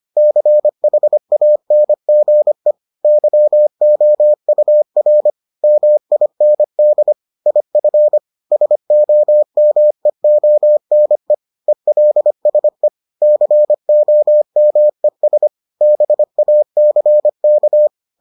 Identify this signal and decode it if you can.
Morse code, heard as a keyed tone, about 25 wpm, 600 Hz. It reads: CHANGE YOUR MIND IF SOMEONE ELSE COMES BACK